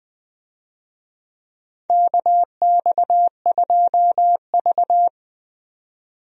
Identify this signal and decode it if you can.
Morse code, heard as a keyed tone, about 20 wpm, 700 Hz. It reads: KX2V
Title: USA Call Signs - 4 Characters - 20wpm